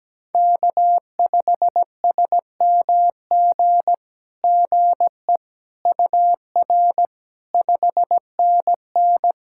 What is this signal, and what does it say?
Morse code, heard as a keyed tone, about 17 wpm, 700 Hz. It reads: K5SMG GE UR 5NN
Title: SOTA QSO Practice 17wpm